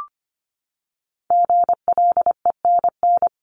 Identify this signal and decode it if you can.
Morse code, heard as a keyed tone, about 25 wpm, 700 Hz. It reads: GLENN
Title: CWT Exchange 25wpm